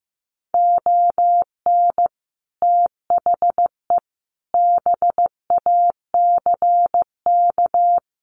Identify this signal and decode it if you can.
Morse code, heard as a keyed tone, about 15 wpm, 700 Hz. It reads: ON THE BACK
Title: Sets of 4 Words 15wpm